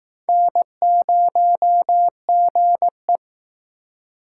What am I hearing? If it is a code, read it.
Morse code, heard as a keyed tone, about 19 wpm, 700 Hz. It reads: N0GE